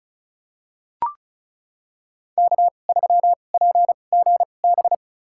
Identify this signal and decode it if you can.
Morse code, heard as a keyed tone, about 34 wpm, spaced as for 25 wpm, 690 Hz. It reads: K3PGB
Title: USA Call Signs - 5 Characters - 2x Character Spacing - 35wpm